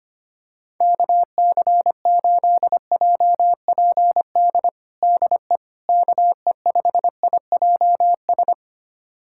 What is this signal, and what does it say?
Morse code, heard as a keyed tone, about 25 wpm, 700 Hz. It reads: KC8JPD DE KE5IJS
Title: USA Callsign DE Callsign - 6 Characters - 25wpm